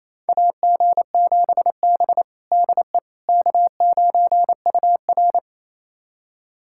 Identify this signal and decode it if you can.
Morse code, heard as a keyed tone, about 28 wpm, 700 Hz. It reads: AG7B DE K9UR
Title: USA Callsign DE Callsign - 4 Characters - 28wpm